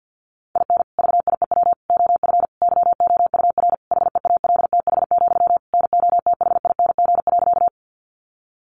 Morse code, heard as a keyed tone, about 50 wpm, 700 Hz. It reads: IN VIEW OF YOUR HEALTH, NOTHING.
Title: Sentences from Top 900 Words 50wpm